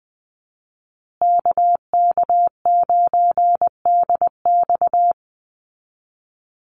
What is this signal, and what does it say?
Morse code, heard as a keyed tone, about 20 wpm, 700 Hz. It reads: KK9DX
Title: Call Signs Repeated 2x - 20wpm